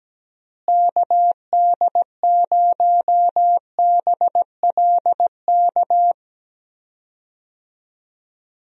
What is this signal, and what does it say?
Morse code, heard as a keyed tone, about 17 wpm, 700 Hz. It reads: KD0BLK
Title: USA Call Signs - 6 Characters - Repeated 3x - 17wpm